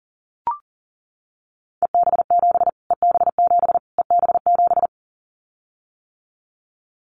Morse code, heard as a keyed tone, about 40 wpm, 700 Hz. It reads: EB7 EB7 EB7
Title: Callsign Prefixes for Top 25 DXCC - Repeated 3x - 40wpm